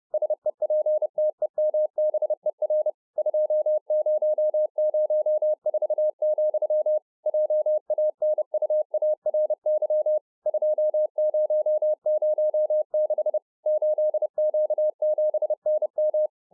Morse code, heard as a keyed tone, about 30 wpm, 610 Hz. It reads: SEPTEMBER 2004, JANUARY 2006 8Q7NM